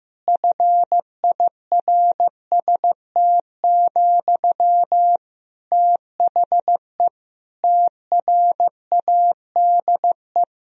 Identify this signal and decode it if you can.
Morse code, heard as a keyed tone, about 15 wpm, 700 Hz. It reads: FIRST, THE TRADE
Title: Sentences from Top 1000 Words - Review Part 4 - 15wpm